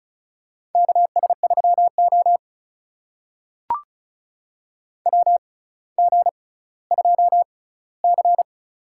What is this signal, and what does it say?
Morse code, heard as a keyed tone, about 34 wpm, spaced as for 18 wpm, 700 Hz. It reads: KS3O WG2C
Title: USA Call Signs - 4 Characters - 6x Character Spacing - 35wpm